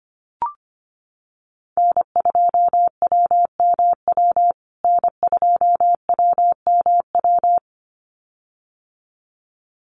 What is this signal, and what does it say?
Morse code, heard as a keyed tone, about 25 wpm, 700 Hz. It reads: N2WMW N2WMW
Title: USA Call Signs Repeated 2x 25wpm